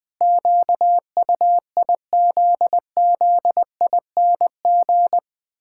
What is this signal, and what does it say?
Morse code, heard as a keyed tone, about 20 wpm, 700 Hz. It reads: QUIZZING
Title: Pangram Challenge v2 - 20wpm